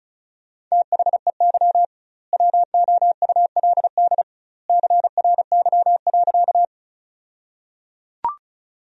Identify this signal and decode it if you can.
Morse code, heard as a keyed tone, about 34 wpm, 700 Hz. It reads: THEY WOULD CRY.
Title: 2x Word-Spacing - Sentences from Top 500 Words - 35wpm